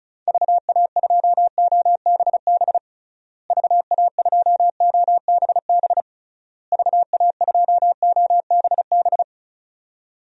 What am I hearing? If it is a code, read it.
Morse code, heard as a keyed tone, about 36 wpm, 700 Hz. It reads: VA2OBB VA2OBB VA2OBB
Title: DXCC-100 Call Signs - 6 Characters - Repeated 3x - 35wpm